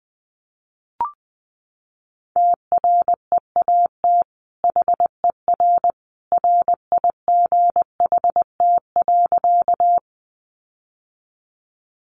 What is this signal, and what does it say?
Morse code, heard as a keyed tone, about 19 wpm, 700 Hz. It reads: TREAT HER RIGHT.
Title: Sentences from Top 1300 Words 20wpm